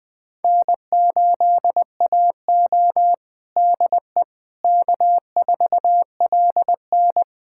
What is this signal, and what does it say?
Morse code, heard as a keyed tone, about 21 wpm, 700 Hz. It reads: N8AO DE K4LN
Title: USA Callsign DE Callsign - 4 Characters - 20wpm